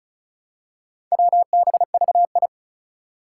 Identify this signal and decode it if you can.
Morse code, heard as a keyed tone, about 36 wpm, 700 Hz. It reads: WBVI
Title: Four Challenging Letter-Number Combinations 35wpm